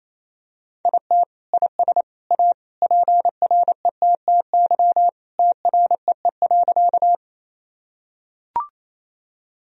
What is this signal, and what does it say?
Morse code, heard as a keyed tone, about 28 wpm, 700 Hz. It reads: IT IS A PRETTY TREE.
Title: Sentences from Top 1000 Words - Review Part 1 - 28wpm